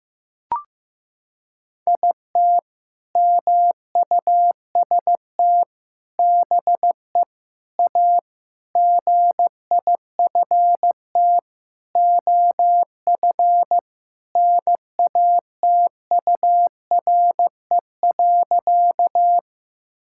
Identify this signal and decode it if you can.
Morse code, heard as a keyed tone, about 15 wpm, 700 Hz. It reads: IT MUST BE A GIFT OF NATURE.
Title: Sentences from Top 1000 Words - Review Part 6 - 15wpm